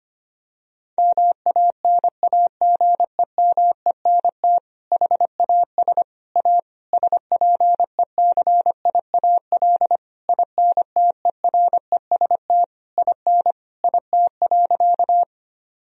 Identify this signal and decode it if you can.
Morse code, heard as a keyed tone, about 25 wpm, 700 Hz. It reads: MANAGEMENT HAS A SPECIAL INTEREST IN IT.
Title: Sentences from Top 1000 Words - Review Part 4 - 25wpm